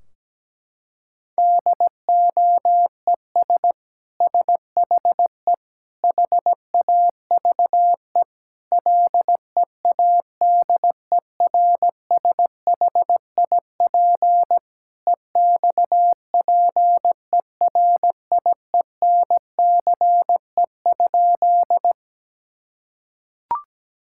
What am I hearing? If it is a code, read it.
Morse code, heard as a keyed tone, about 17 wpm, 700 Hz. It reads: DOES SHE HAVE LEADERSHIP EXPERIENCE?
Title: Sentences from Top 1600 Words 17wpm